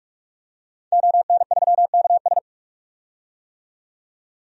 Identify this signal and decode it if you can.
Morse code, heard as a keyed tone, about 44 wpm, 690 Hz. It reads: ON3KS
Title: DXCC-100 Call Signs - 5 Characters - 45wpm